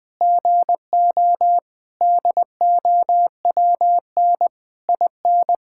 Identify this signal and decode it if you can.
Morse code, heard as a keyed tone, about 20 wpm, 700 Hz. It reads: GO DOWN IN
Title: Sentences with 9-Word Phrases - Spoken Beginning - 20wpm